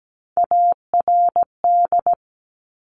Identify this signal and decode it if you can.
Morse code, heard as a keyed tone, about 17 wpm, 700 Hz. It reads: ARD
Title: Top Three Letter Combinations in Words 17wpm